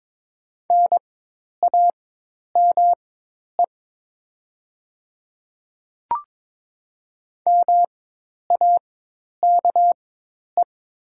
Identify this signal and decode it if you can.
Morse code, heard as a keyed tone, about 22 wpm, spaced as for 10 wpm, 700 Hz. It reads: NAME MAKE
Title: 4x Character Spacing - Top 300 Words - 22wpm